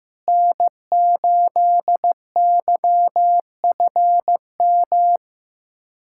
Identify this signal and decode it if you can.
Morse code, heard as a keyed tone, about 15 wpm, 700 Hz. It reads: N8YFM
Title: USA Callsign DE Callsign - 5 Characters - 15wpm